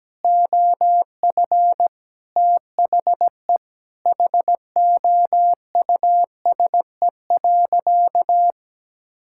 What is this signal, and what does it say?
Morse code, heard as a keyed tone, about 17 wpm, 700 Hz. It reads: OF THE HOUSE.
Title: Sentences from Top 1000 Words - Review Part 5 - 17wpm